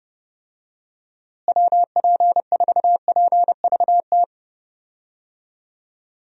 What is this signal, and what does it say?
Morse code, heard as a keyed tone, about 30 wpm, 700 Hz. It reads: WP4PVT
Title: USA Call Signs - 6 Characters - Repeated 2x - 30wpm